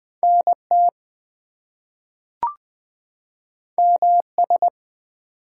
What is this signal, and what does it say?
Morse code, heard as a keyed tone, about 20 wpm, 700 Hz. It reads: NT MS
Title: US States and Canadian Provinces 20wpm